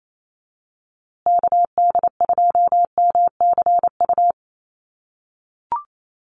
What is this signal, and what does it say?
Morse code, heard as a keyed tone, about 28 wpm, 700 Hz. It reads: KD2MCU
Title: USA Call Signs Repeated 2x 28wpm